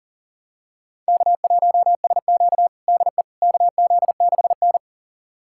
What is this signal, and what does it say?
Morse code, heard as a keyed tone, about 37 wpm, 700 Hz. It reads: K1SQ DE KZ6N
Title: USA Callsign DE Callsign - 4 Characters - 40wpm